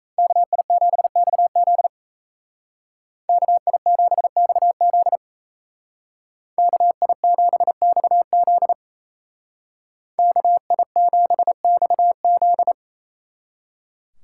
Morse code, heard as a keyed tone, about 36 wpm, 700 Hz. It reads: KI7XZ KI7XZ KI7XZ KI7XZ